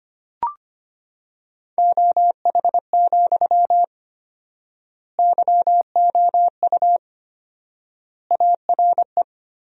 Morse code, heard as a keyed tone, about 25 wpm, 700 Hz. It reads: OH, YOU ARE
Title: 4x Word-Spacing - Sentences from Top 500 Words - 25wpm